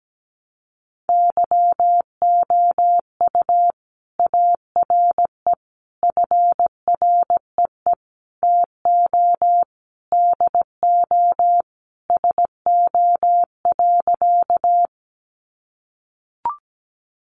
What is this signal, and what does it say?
Morse code, heard as a keyed tone, about 17 wpm, 700 Hz. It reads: YOU ARE FREE TO DO SO.
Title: Sentences from Top 500 Words 17wpm